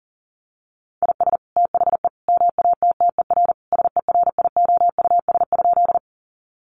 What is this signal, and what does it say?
Morse code, heard as a keyed tone, about 37 wpm, 700 Hz. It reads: IS THE MATTER SERIOUS?
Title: Sentences from Top 1100 Words 40wpm